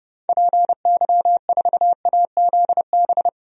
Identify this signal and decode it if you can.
Morse code, heard as a keyed tone, about 32 wpm, 700 Hz. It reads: PY4AZB
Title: DXCC-100 Call Signs - 6 Characters - Repeated 3x - 30wpm